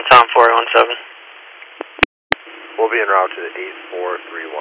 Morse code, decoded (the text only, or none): none